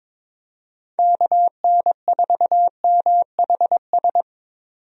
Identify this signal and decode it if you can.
Morse code, heard as a keyed tone, about 21 wpm, 700 Hz. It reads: KN4MHS
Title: USA Call Signs - 6 Characters - 22wpm